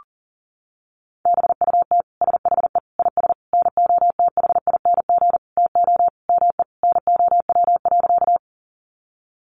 Morse code, heard as a keyed tone, about 40 wpm, 700 Hz. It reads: BUT SHE IS NOTHING TO ME NOW.
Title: Sentences from Top 500 Words 40wpm